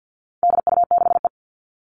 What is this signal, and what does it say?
Morse code, heard as a keyed tone, about 50 wpm, 700 Hz. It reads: DU6E
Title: All Cut Numbers - 4-Digit 50wpm